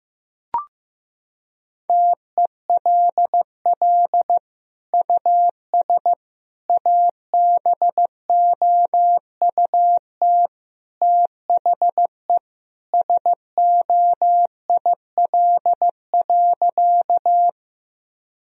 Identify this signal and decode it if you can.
Morse code, heard as a keyed tone, about 15 wpm, 700 Hz. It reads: TELL US ABOUT THE SOIL.
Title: Sentences from Top 1000 Words - Review Part 4 - 15wpm